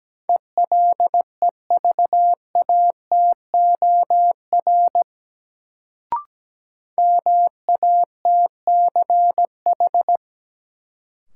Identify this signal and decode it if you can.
Morse code, heard as a keyed tone, about 17 wpm, 700 Hz. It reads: ELEVATOR MATCH
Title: Sentences from Top 1600 Words - New Word Drill - 17wpm